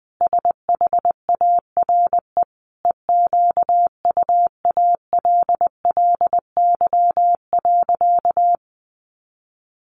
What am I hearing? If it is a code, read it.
Morse code, heard as a keyed tone, about 20 wpm, 700 Hz. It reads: SHARE EQUALLY.
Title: Sentences from Top 1400 Words 20wpm